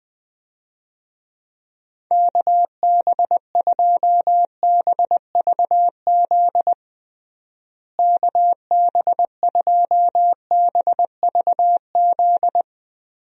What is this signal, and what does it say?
Morse code, heard as a keyed tone, about 20 wpm, 700 Hz. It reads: KB2BVZ KB2BVZ
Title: USA Call Signs - 6 Characters - Repeated 2x - 20wpm